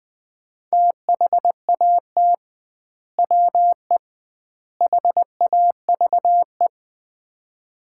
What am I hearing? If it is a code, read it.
Morse code, heard as a keyed tone, about 20 wpm, 700 Hz. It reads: THAT WE HAVE
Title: 2x Word-Spacing - Sets of 3 Words - 20wpm